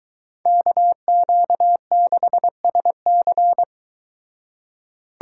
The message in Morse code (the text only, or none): KQ6SC